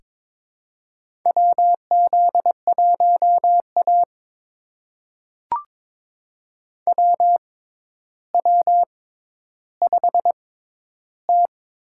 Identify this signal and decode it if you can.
Morse code, heard as a keyed tone, about 22 wpm, spaced as for 11 wpm, 700 Hz. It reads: WZ1A WW5T
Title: USA Call Signs - 6x Character Spacing - 22wpm